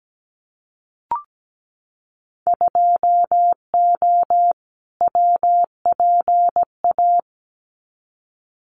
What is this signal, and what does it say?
Morse code, heard as a keyed tone, about 17 wpm, 700 Hz. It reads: 2O WPA
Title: Winter Field Day Exchanges Encore 17wpm